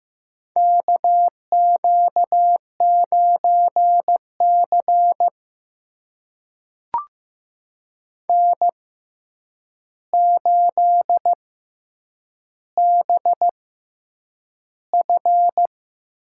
Morse code, heard as a keyed tone, about 15 wpm, spaced as for 8 wpm, 700 Hz. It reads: KQ9C N8BF
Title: USA Call Signs - 4 Characters - 6x Character Spacing - 15wpm